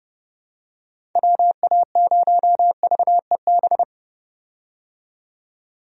Morse code, heard as a keyed tone, about 30 wpm, 700 Hz. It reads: WA0VEB
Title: USA Call Signs - 6 Characters - 30wpm